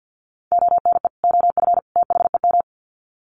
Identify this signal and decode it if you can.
Morse code, heard as a keyed tone, about 50 wpm, 700 Hz. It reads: ONE OF THEM
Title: Sets of 3 Words 50wpm